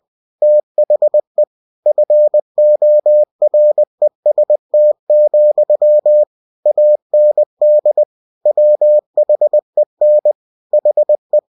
Morse code, heard as a keyed tone, about 20 wpm, 600 Hz. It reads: THE FOREST, AND WHEN HE
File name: cw-books / winnie-the-pooh / winnie-20wpm-page-0017